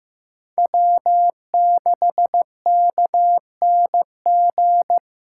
Morse code, heard as a keyed tone, about 15 wpm, 700 Hz. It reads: W6KNG
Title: USA Call Signs - 5 Characters - 2x Character Spacing - 15wpm